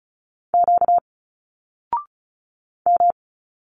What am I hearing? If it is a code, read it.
Morse code, heard as a keyed tone, about 33 wpm, 700 Hz. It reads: Q M
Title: Single Letters 35wpm